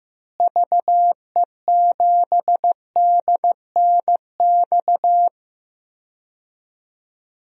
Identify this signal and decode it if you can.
Morse code, heard as a keyed tone, about 16 wpm, 700 Hz. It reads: VE7DNX
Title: DXCC-100 Call Signs - 6 Characters - Repeated 2x - 15wpm